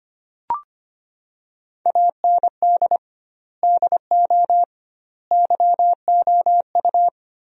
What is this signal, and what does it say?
Morse code, heard as a keyed tone, about 25 wpm, 700 Hz. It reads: AND DO YOU
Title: 2x Word-Spacing - Sentences from Top 100 Words - 25wpm